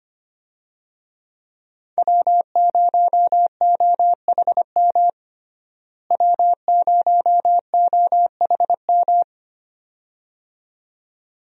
Morse code, heard as a keyed tone, about 25 wpm, 700 Hz. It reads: W0OHM W0OHM
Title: USA Call Signs - 5 Characters - Repeated 2x - 25wpm